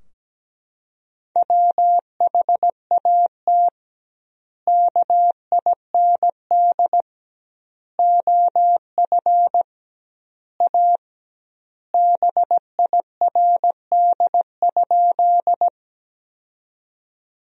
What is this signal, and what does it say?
Morse code, heard as a keyed tone, about 17 wpm, 700 Hz. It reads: WHAT KIND OF A BIRD?
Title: 2x Word-Spacing - Sentences from Top 400 Words - 17wpm